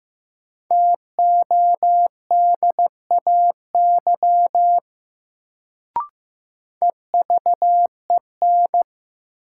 Morse code, heard as a keyed tone, about 15 wpm, 700 Hz. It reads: TODAY EVEN
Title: Sentences from Top 600 Words - New Word Drill - 15wpm